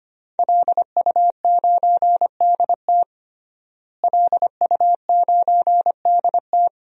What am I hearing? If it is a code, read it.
Morse code, heard as a keyed tone, about 25 wpm, 700 Hz. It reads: LU9DT LU9DT
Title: DXCC-100 Call Signs - 5 Characters - Repeated 2x - 25wpm